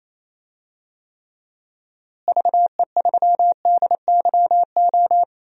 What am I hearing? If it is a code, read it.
Morse code, heard as a keyed tone, about 27 wpm, 700 Hz. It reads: VE3DYO
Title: DXCC-100 Call Signs - 6 Characters - Repeated 3x - 28wpm